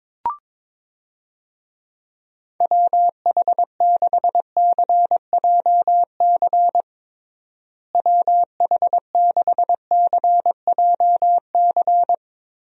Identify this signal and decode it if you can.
Morse code, heard as a keyed tone, about 22 wpm, 700 Hz. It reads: WH6CJC WH6CJC
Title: USA Call Signs - 6 Characters - Repeated 3x - 22wpm